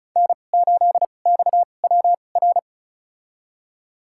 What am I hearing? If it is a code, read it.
Morse code, heard as a keyed tone, about 36 wpm, spaced as for 25 wpm, 700 Hz. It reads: N8XWR
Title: USA Call Signs - 5 Characters - 2x Character Spacing - 35wpm